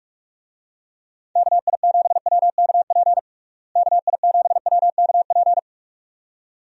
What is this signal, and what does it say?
Morse code, heard as a keyed tone, about 46 wpm, 700 Hz. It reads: KI7WKP KI7WKP
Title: USA Call Signs - 6 Characters - Repeated 2x - 45wpm